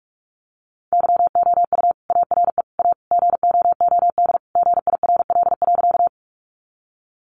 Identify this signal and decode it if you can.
Morse code, heard as a keyed tone, about 44 wpm, 700 Hz. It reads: YOU ARE A GOOD GIRL.
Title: Sentences from Top 400 Words 45wpm